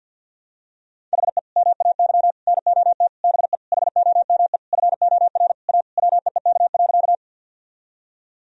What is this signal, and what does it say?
Morse code, heard as a keyed tone, about 46 wpm, 700 Hz. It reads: HE MAY NOT BE HOME FOR A WEEK.
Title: Sentences from Top 1000 Words - Review Part 2 - 50wpm